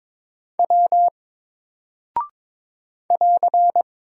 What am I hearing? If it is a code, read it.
Morse code, heard as a keyed tone, about 22 wpm, 700 Hz. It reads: W <AR>
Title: Single Character (Letter:Number:Punctuation:Prosign) Alt 22wpm